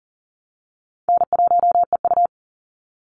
Morse code, heard as a keyed tone, about 40 wpm, 700 Hz. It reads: N1EU
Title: NA SOTA Activator Callsigns 40wpm